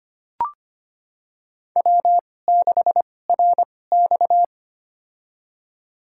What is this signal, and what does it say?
Morse code, heard as a keyed tone, about 25 wpm, spaced as for 18 wpm, 700 Hz. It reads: W6RX
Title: USA Call Signs - 4 Characters - 2x Character Spacing - 25wpm